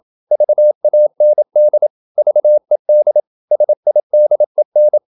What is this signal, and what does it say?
Morse code, heard as a keyed tone, about 27 wpm, 600 Hz. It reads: VAND VED SIDEN
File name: Ferie QRQ 27wpm